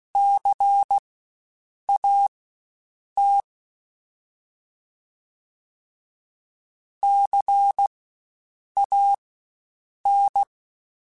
Morse code, heard as a keyed tone, about 16 wpm, spaced as for 7 wpm, 800 Hz. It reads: CAT CAN
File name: Spaced Alphas / Spaced Alphas mp3s / Cs sp @15